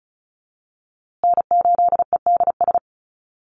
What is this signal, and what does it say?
Morse code, heard as a keyed tone, about 34 wpm, 700 Hz. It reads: N8EDS
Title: USA Call Signs Repeated 2x 35wpm